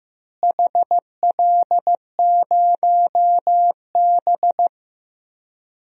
Morse code, heard as a keyed tone, about 16 wpm, 700 Hz. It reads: HL0B
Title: DXCC-100 Call Signs - 4 Characters - Repeated 2x - 15wpm